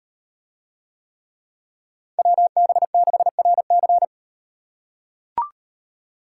Morse code, heard as a keyed tone, about 38 wpm, 700 Hz. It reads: WB6RC